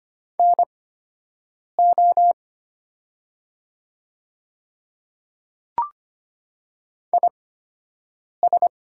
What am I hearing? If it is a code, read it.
Morse code, heard as a keyed tone, about 25 wpm, spaced as for 7 wpm, 700 Hz. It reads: NO IS
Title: 8x Character Spacing - Top 100 Words - 25wpm